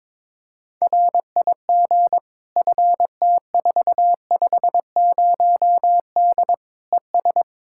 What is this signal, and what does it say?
Morse code, heard as a keyed tone, about 22 wpm, 700 Hz. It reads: RIG FT450D ES
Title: Practice QSO - Exchange 2 - Info - 22wpm